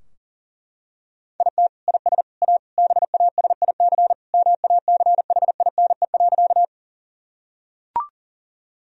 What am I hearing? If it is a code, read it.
Morse code, heard as a keyed tone, about 37 wpm, 700 Hz. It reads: IT IS A BASIC MACHINE.
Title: Sentences from Top 1000 Words - Review Part 3 - 40wpm